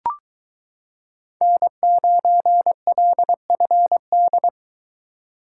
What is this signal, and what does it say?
Morse code, heard as a keyed tone, about 22 wpm, 700 Hz. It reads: N9LFD